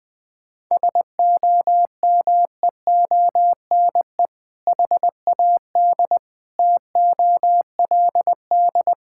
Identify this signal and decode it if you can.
Morse code, heard as a keyed tone, about 21 wpm, 700 Hz. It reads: SOMEONE HAD TOLD
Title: Sentences from Top 1000 Words - Review Part 6 - 20wpm